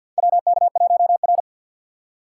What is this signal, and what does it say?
Morse code, heard as a keyed tone, about 48 wpm, 700 Hz. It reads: WK1R
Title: USA Call Signs - 6x Character Spacing - 50wpm